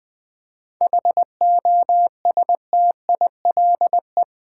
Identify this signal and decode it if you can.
Morse code, heard as a keyed tone, about 20 wpm, 690 Hz. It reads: HOSTILE
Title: 7 Letter Words - First 3 Letters Given - 20wpm